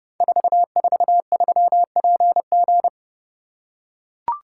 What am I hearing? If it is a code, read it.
Morse code, heard as a keyed tone, about 30 wpm, 700 Hz. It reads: 443PG
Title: Code Groups - 5 Letter:Numbers - 30wpm